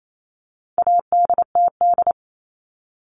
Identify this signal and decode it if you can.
Morse code, heard as a keyed tone, about 28 wpm, 700 Hz. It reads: ADTD
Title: All Cut Numbers - 4-Digit 28wpm